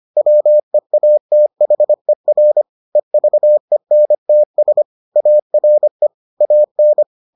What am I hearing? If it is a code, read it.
Morse code, heard as a keyed tone, about 25 wpm, 600 Hz. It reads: WEATHER EVENTS ARE AN